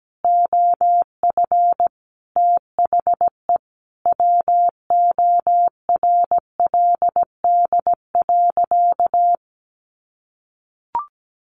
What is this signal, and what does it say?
Morse code, heard as a keyed tone, about 17 wpm, 700 Hz. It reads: OF THE WORLD.
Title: Sentences from Top 1400 Words 17wpm